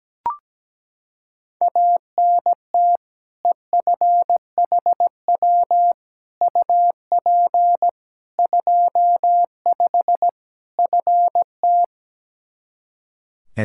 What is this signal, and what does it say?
Morse code, heard as a keyed tone, about 17 wpm, 700 Hz. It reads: ANT EFHW UP 25 FT
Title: K1USN SST QSO Practice 17wpm